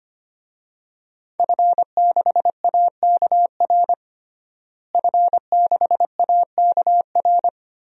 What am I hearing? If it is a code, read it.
Morse code, heard as a keyed tone, about 25 wpm, 700 Hz. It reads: F6AKR F6AKR
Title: DXCC-100 Call Signs - 5 Characters - Repeated 2x - 25wpm